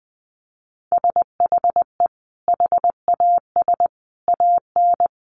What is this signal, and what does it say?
Morse code, heard as a keyed tone, about 20 wpm, 700 Hz. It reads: SHE HAS AN
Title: Sentences from Top 800 Words 20wpm